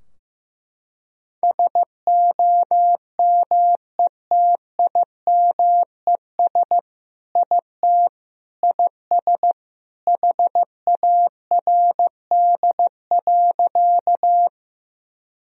Morse code, heard as a keyed tone, about 15 wpm, 700 Hz. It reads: SOMETIMES IT IS HARD.
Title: Sentences from Top 1000 Words - Review Part 3 - 15wpm